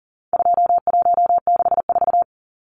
Morse code, heard as a keyed tone, about 40 wpm, 700 Hz. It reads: 2164
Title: Numbers - 4-Digit 40wpm